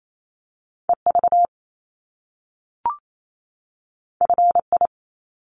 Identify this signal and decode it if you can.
Morse code, heard as a keyed tone, about 28 wpm, 700 Hz. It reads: EV FI
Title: Top Two Letter Combinations in Words 28wpm